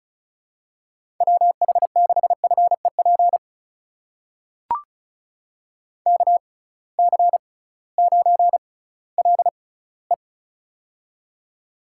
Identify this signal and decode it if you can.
Morse code, heard as a keyed tone, about 36 wpm, spaced as for 19 wpm, 700 Hz. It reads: WH6FEP KC9LE
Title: USA Call Signs - 6x Character Spacing - 35wpm